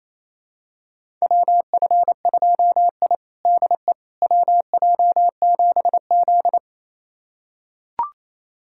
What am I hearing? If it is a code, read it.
Morse code, heard as a keyed tone, about 28 wpm, 700 Hz. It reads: WF2I DE WJ7Z
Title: USA Callsign DE Callsign - 4 Characters - 28wpm